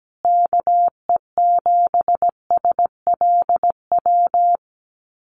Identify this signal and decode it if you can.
Morse code, heard as a keyed tone, about 17 wpm, 700 Hz. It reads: KE7SLW
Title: USA Call Signs Repeated 2x 17wpm